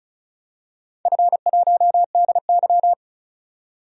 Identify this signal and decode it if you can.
Morse code, heard as a keyed tone, about 36 wpm, 700 Hz. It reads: F1DY
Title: DXCC-100 Call Signs - 4 Characters - 35wpm